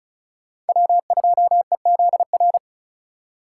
Morse code, heard as a keyed tone, about 36 wpm, 700 Hz. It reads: W2EZR
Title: USA Call Signs - 5 Characters - 4x Character Spacing - 35wpm